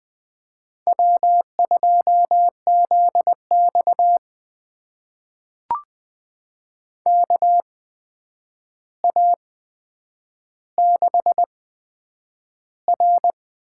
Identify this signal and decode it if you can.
Morse code, heard as a keyed tone, about 20 wpm, spaced as for 9 wpm, 700 Hz. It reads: W2ZX KA6R
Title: USA Call Signs - 4 Characters - 8x Character Spacing - 20wpm